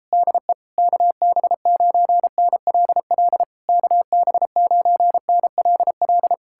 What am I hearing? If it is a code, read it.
Morse code, heard as a keyed tone, about 37 wpm, 700 Hz. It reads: DE KB9NLL KB9NLL